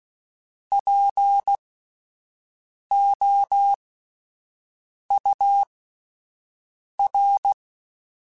Morse code, heard as a keyed tone, about 16 wpm, spaced as for 5 wpm, 800 Hz. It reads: POUR